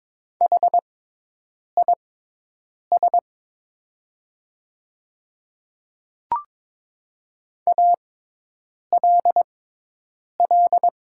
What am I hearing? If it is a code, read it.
Morse code, heard as a keyed tone, about 22 wpm, spaced as for 8 wpm, 700 Hz. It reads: HIS ALL
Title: 6x Character Spacing - Top 100 Words - 22wpm